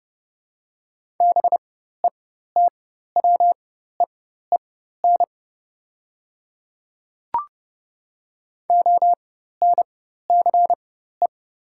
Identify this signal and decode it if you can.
Morse code, heard as a keyed tone, about 30 wpm, spaced as for 14 wpm, 700 Hz. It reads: BETWEEN ONCE
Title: 4x Character Spacing - Top 300 Words - 30wpm